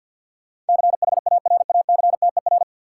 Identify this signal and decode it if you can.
Morse code, heard as a keyed tone, about 52 wpm, 700 Hz. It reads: CHARACTER
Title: Sentences from Top 1200 Words - New Word Drill - 50wpm